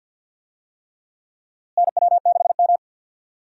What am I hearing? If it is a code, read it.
Morse code, heard as a keyed tone, about 52 wpm, 700 Hz. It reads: NW6M
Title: USA Call Signs - 4 Characters - Repeated 3x - 50wpm